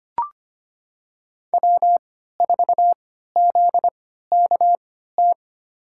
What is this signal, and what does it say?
Morse code, heard as a keyed tone, about 25 wpm, spaced as for 14 wpm, 700 Hz. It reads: W4ZKT
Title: USA Call Signs - 3x Character Spacing - 25wpm